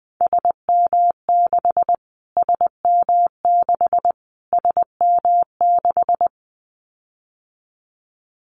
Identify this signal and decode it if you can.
Morse code, heard as a keyed tone, about 20 wpm, 700 Hz. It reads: SM6 SM6 SM6
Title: Callsign Prefixes for Top 25 DXCC - Repeated 3x - 20wpm